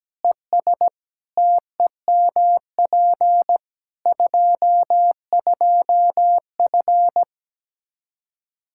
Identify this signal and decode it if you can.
Morse code, heard as a keyed tone, about 17 wpm, 700 Hz. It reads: ES TEMP 22F
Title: Practice QSO - Exchange 2 - Info - 17wpm